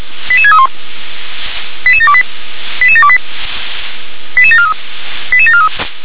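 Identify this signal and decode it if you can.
Five-tone selective call (ZVEI), heard as quick runs of five tones, tones of about 70 ms. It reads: 88631 88728 88728 88743 88743
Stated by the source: ZVEI five-tone call 88730 received by SDR.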